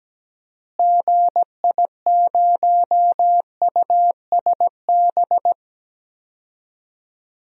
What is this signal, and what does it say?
Morse code, heard as a keyed tone, about 17 wpm, 700 Hz. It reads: GI0USB
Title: DXCC-100 Call Signs - 6 Characters - Repeated 3x - 17wpm